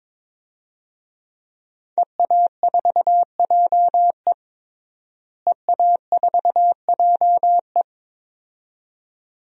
Morse code, heard as a keyed tone, about 22 wpm, 700 Hz. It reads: EA4JE EA4JE
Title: DXCC-100 Call Signs - 5 Characters - Repeated 2x - 22wpm